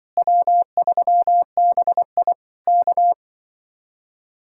Morse code, heard as a keyed tone, about 23 wpm, 700 Hz. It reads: W3BI K
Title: Speed-Racing Common QSO Phrases Encore - 20wpm